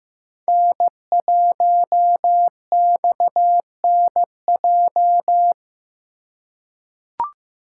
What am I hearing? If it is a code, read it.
Morse code, heard as a keyed tone, about 15 wpm, 700 Hz. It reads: N1XNJ
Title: USA Call Signs - 5 Characters - 3x Character Spacing - 15wpm